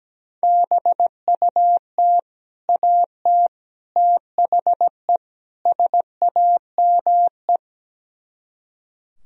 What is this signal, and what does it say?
Morse code, heard as a keyed tone, about 17 wpm, 700 Hz. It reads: BUT AT THE SAME
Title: Speed-Racing - Sets of 4 Words 17wpm